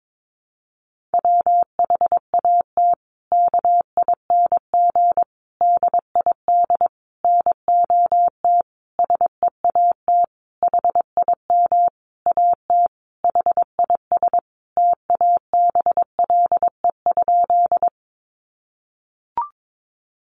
Morse code, heard as a keyed tone, about 22 wpm, 700 Hz. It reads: WHAT KING DID NOT SEAT HIM AT HIS TABLE?
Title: Sentences from Top 1500 Words 22wpm